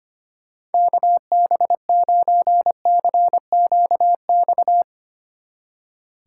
Morse code, heard as a keyed tone, about 33 wpm, 700 Hz. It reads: KB9CQX